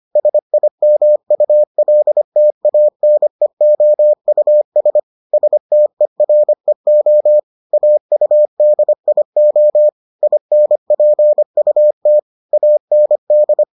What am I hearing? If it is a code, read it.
Morse code, heard as a keyed tone, about 25 wpm, 600 Hz. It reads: SIMULTANEOUS STEREO AUDIO INPUT AND